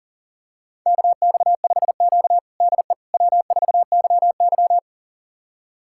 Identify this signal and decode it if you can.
Morse code, heard as a keyed tone, about 37 wpm, 700 Hz. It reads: KX5Q DE W4YY
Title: USA Callsign DE Callsign - 4 Characters - 40wpm